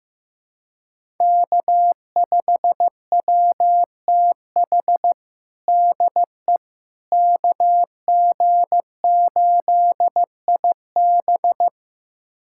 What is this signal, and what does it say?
Morse code, heard as a keyed tone, about 15 wpm, 700 Hz. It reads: K5WTH DE KG8IB
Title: USA Callsign DE Callsign - 5 Characters - 15wpm